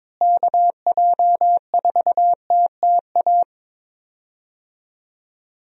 Morse code, heard as a keyed tone, about 22 wpm, 700 Hz. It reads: KJ4TTA
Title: USA Call Signs - 6 Characters - Repeated 3x - 22wpm